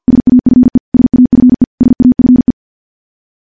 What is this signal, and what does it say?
Morse code, heard as a keyed tone, about 50 wpm, 250 Hz. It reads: SAFE SAFE SAFE
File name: morse / morse-training-50WPM